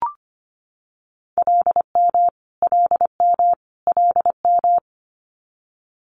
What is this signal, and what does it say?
Morse code, heard as a keyed tone, about 25 wpm, 700 Hz. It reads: LM LM LM
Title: Multi-letter Phonemes Repeated 3x 25wpm